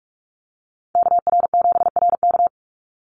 Morse code, heard as a keyed tone, about 44 wpm, 700 Hz. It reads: KR7RK
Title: NA SOTA Chaser Callsigns 45wpm